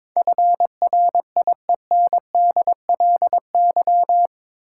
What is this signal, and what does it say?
Morse code, heard as a keyed tone, about 22 wpm, 700 Hz. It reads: FRIENDLY
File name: practice / 22wpm / Eight Letter Words - 22wpm